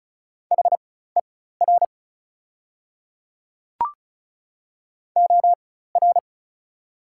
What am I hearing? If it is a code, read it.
Morse code, heard as a keyed tone, about 36 wpm, spaced as for 16 wpm, 700 Hz. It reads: HER OR
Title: 4x Character Spacing - Top 300 Words - 35wpm